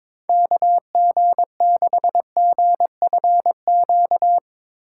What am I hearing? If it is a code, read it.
Morse code, heard as a keyed tone, about 22 wpm, 700 Hz. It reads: KG6GFQ
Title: USA Call Signs - 6 Characters - 3x Character Spacing - 22wpm